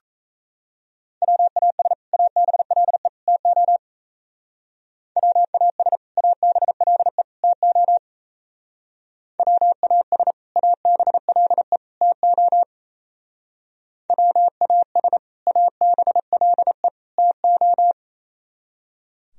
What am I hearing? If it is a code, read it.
Morse code, heard as a keyed tone, about 33 wpm, 700 Hz. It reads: WAS ABLE TO WAS ABLE TO WAS ABLE TO WAS ABLE TO
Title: Speed-Racing - Sets of 3 Words 28wpm